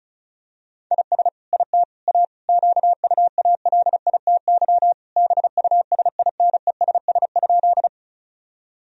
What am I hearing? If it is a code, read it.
Morse code, heard as a keyed tone, about 34 wpm, 700 Hz. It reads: IS IT A QUALITY BUSINESS?
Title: Sentences from Top 1000 Words - Review Part 4 - 35wpm